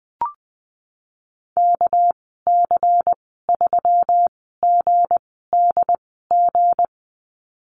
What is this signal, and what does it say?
Morse code, heard as a keyed tone, about 20 wpm, spaced as for 15 wpm, 700 Hz. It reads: KC3GDG
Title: 213 Lesson 34g - Number 8 - Call Signs 20wpm